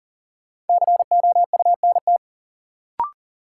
Morse code, heard as a keyed tone, about 40 wpm, 700 Hz. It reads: COUNT
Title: Sentences from Top 1500 Words - New Word Drill - 40wpm